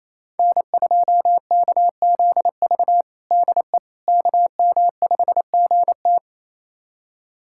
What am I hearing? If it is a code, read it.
Morse code, heard as a keyed tone, about 28 wpm, 700 Hz. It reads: N2KZV DE KM5GT
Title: USA Callsign DE Callsign - 5 Characters - 28wpm